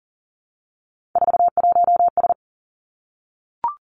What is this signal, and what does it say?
Morse code, heard as a keyed tone, about 38 wpm, 700 Hz. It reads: 41S
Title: Three Challenging Letter-Number Combinations 40wpm